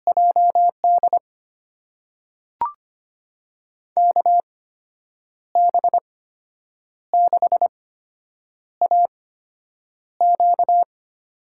Morse code, heard as a keyed tone, about 25 wpm, spaced as for 8 wpm, 700 Hz. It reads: JD KB6AQ
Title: USA Call Signs - 6 Characters - 8x Character Spacing - 25wpm